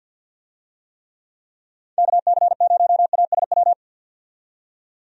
Morse code, heard as a keyed tone, about 51 wpm, 700 Hz. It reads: KC0ASW
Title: USA Call Signs - 6 Characters - 50wpm